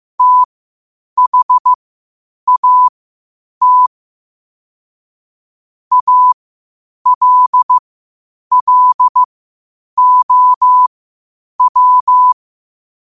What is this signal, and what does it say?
Morse code, heard as a keyed tone, about 15 wpm, spaced as for 8 wpm, 1000 Hz. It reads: THAT ALLOW
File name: qotdfiles / 8wpm